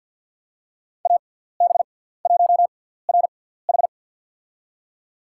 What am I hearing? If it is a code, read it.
Morse code, heard as a keyed tone, about 47 wpm, spaced as for 17 wpm, 700 Hz. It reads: AB1RH
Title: USA Call Signs - 5 Characters - 6x Character Spacing - 50wpm